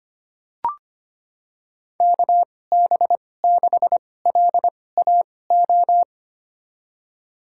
Morse code, heard as a keyed tone, about 25 wpm, spaced as for 18 wpm, 700 Hz. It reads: KB6LAO
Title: USA Call Signs - 6 Characters - 2x Character Spacing - 25wpm